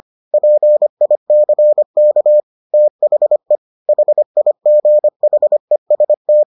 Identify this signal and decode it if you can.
Morse code, heard as a keyed tone, about 25 wpm, 600 Hz. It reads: PICK THE HIGHEST